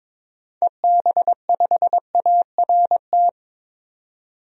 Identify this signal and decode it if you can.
Morse code, heard as a keyed tone, about 22 wpm, 700 Hz. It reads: EB5ART
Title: DXCC-100 Call Signs - 6 Characters - 22wpm